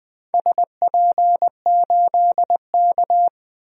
Morse code, heard as a keyed tone, about 20 wpm, 700 Hz. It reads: SP8K
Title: DXCC-100 Call Signs - 4 Characters - 20wpm